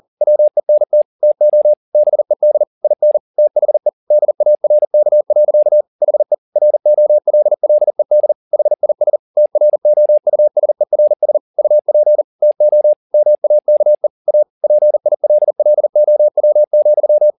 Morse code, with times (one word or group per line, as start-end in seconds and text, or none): 0.2-1.0 WENT
1.2-1.7 TO
1.9-2.6 BED
2.8-3.2 IN
3.4-3.9 THE
4.1-5.8 DARK.
6.0-6.3 HE
6.6-8.3 ROLLED
8.5-9.2 HIS
9.4-11.4 TROUSERS
11.6-12.2 UP
12.4-12.9 TO
13.1-14.1 MAKE
14.3-14.4 A
14.6-17.3 PILLOW,